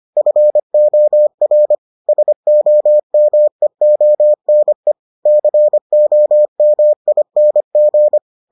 Morse code, heard as a keyed tone, about 25 wpm, 600 Hz. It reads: FOR SOMEONE COMING